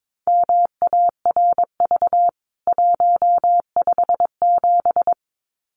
Morse code, heard as a keyed tone, about 22 wpm, 700 Hz. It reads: MARV 157
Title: CWT Exchange 22wpm